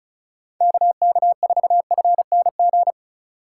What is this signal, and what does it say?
Morse code, heard as a keyed tone, about 36 wpm, 700 Hz. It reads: KK4FNG
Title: USA Call Signs - 6 Characters - 4x Character Spacing - 35wpm